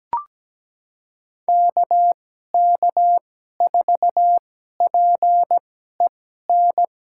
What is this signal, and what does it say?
Morse code, heard as a keyed tone, about 17 wpm, spaced as for 12 wpm, 700 Hz. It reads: KK4PEN
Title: USA Call Signs - 2x Character Spacing - 17wpm